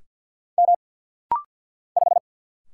Morse code, heard as a keyed tone, about 50 wpm, 700 Hz. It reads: M 5
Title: Single Letter-Number - Rapid-Fire 50wpm